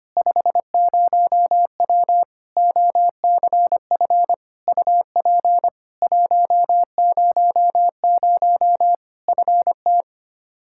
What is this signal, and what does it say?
Morse code, heard as a keyed tone, about 25 wpm, 700 Hz. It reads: 50W OCF UP 100 FT
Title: Practice QSO - Exchange 2 - Info - 25wpm